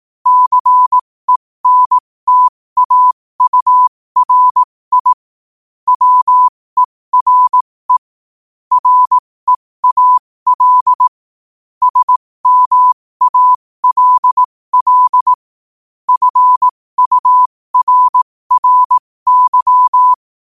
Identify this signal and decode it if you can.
Morse code, heard as a keyed tone, about 18 wpm, spaced as for 15 wpm, 1000 Hz. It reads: CENTAURI WERE REAL SMALL FURRY
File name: qotdfiles / 15wpm